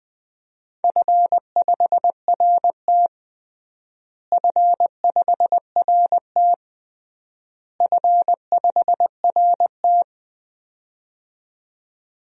Morse code, heard as a keyed tone, about 20 wpm, 700 Hz. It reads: F5RT F5RT F5RT
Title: DXCC-100 Call Signs - 4 Characters - Repeated 3x - 20wpm